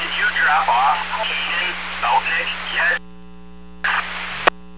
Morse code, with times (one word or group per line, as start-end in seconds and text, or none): none